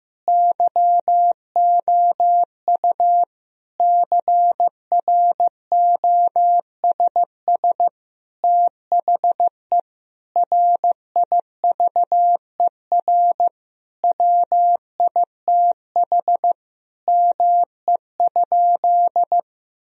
Morse code, heard as a keyed tone, about 15 wpm, 700 Hz. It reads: YOU CROSS THE RIVER WITH ME?
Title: Sentences from Top 1000 Words - Review Part 3 - 15wpm